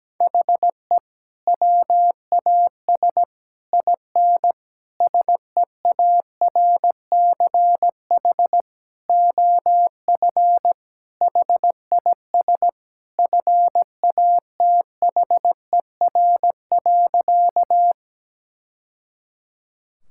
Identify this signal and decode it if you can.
Morse code, heard as a keyed tone, about 17 wpm, 700 Hz. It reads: HE WAS IN SEARCH OF HIS FATHER.
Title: Sentences from Top 1000 Words - Review Part 5 - 17wpm